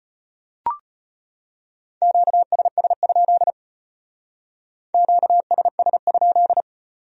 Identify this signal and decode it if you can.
Morse code, heard as a keyed tone, about 33 wpm, 700 Hz. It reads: QSS? QSS?